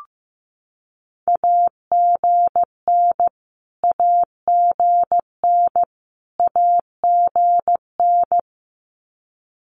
Morse code, heard as a keyed tone, about 15 wpm, 700 Hz. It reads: AGN AGN AGN
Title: Abbreviations Repeated 3x - 15wpm